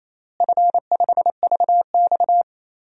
Morse code, heard as a keyed tone, about 28 wpm, 700 Hz. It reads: F5VX
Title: DXCC-100 Call Signs - 4 Characters - Repeated 2x - 28wpm